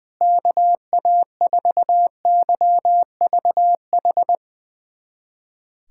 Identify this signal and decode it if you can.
Morse code, heard as a keyed tone, about 20 wpm, 700 Hz. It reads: KA4YVH
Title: USA Callsign DE Callsign 20wpm